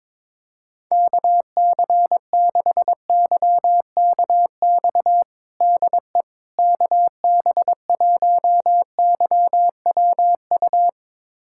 Morse code, heard as a keyed tone, about 21 wpm, 700 Hz. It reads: KC6YKX DE KB1YWU
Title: USA Callsign DE Callsign - 6 Characters - 22wpm